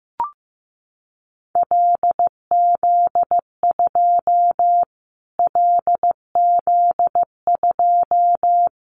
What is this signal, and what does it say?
Morse code, heard as a keyed tone, about 15 wpm, 700 Hz. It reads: LZ2 LZ2
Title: Callsign Prefixes for Top 25 DXCC - Repeated 3x - 15wpm